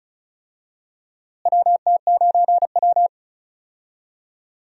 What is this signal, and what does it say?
Morse code, heard as a keyed tone, about 36 wpm, 700 Hz. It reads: WT9W
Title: USA Call Signs - 4 Characters - Repeated 3x - 35wpm